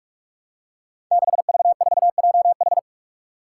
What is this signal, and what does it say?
Morse code, heard as a keyed tone, about 46 wpm, 700 Hz. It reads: 6V4JH
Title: Five Challenging Letter-Number Combinations 45wpm